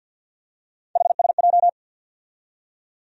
Morse code, heard as a keyed tone, about 52 wpm, 700 Hz. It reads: HSJ
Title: USA Call Sign Postfixes - 3 Letters 50wpm